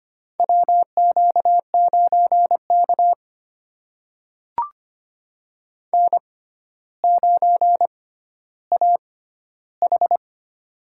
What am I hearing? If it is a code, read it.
Morse code, heard as a keyed tone, about 25 wpm, spaced as for 13 wpm, 700 Hz. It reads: WQ9K N9AH
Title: USA Call Signs - 4 Characters - 6x Character Spacing - 25wpm